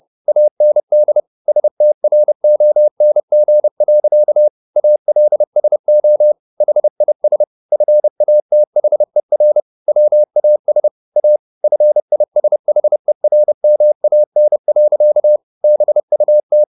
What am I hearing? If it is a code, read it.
Morse code, heard as a keyed tone, about 30 wpm, 600 Hz. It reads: AND STRONG. ALSO HIS FATHER WAS A FISHERMAN. BUT
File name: the old man and the sea / the old man and the sea 1 30wpm